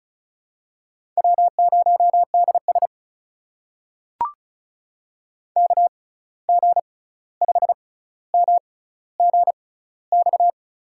Morse code, heard as a keyed tone, about 34 wpm, spaced as for 16 wpm, 700 Hz. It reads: W0DS KG5MGX
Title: USA Call Signs - 6x Character Spacing - 35wpm